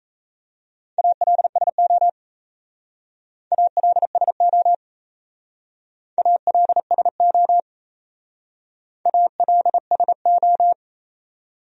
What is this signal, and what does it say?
Morse code, heard as a keyed tone, about 35 wpm, 700 Hz. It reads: ALSO ALSO ALSO ALSO
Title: Speed-Racing Top 200 Words 28wpm